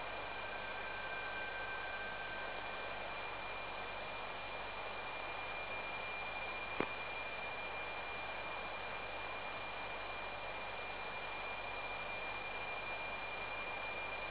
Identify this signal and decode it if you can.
PSK31, about 31.25 baud.